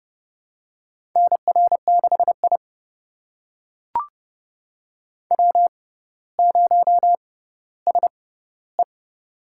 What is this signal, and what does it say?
Morse code, heard as a keyed tone, about 30 wpm, spaced as for 15 wpm, 700 Hz. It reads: NR6I W0SE
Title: USA Call Signs - 4 Characters - 6x Character Spacing - 30wpm